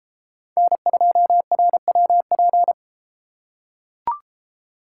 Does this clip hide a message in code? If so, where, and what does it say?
Yes: Morse code from 0.6 to 2.7 s, heard as a keyed tone, about 32 wpm, 700 Hz. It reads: N2RWP